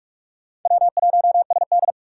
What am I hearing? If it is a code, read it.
Morse code, heard as a keyed tone, about 46 wpm, 700 Hz. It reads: W1SD
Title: USA Call Signs - 4 Characters - 3x Character Spacing - 45wpm